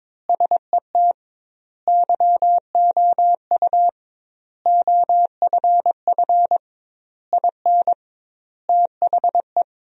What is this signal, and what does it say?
Morse code, heard as a keyed tone, about 22 wpm, 700 Hz. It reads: SET YOU OFF IN THE
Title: 2x Word-Spacing - Sentences from Top 200 Words - 22wpm